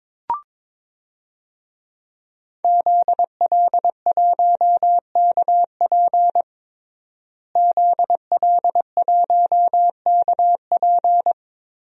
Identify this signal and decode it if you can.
Morse code, heard as a keyed tone, about 22 wpm, 700 Hz. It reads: ZL1KP ZL1KP
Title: DXCC-100 Call Signs - 5 Characters - Repeated 3x - 22wpm